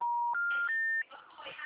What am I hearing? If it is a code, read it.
Special information tone, with noise.